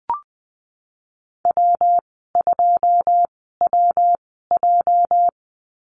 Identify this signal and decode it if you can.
Morse code, heard as a keyed tone, about 19 wpm, spaced as for 14 wpm, 700 Hz. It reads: W2WJ
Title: 207 Lesson 33g - Letter j - Call Signs 20wpm